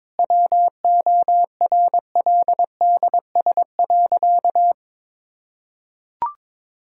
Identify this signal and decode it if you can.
Morse code, heard as a keyed tone, about 22 wpm, 700 Hz. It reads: WORLDS.
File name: Difficult Sentences with Hints - 22wpm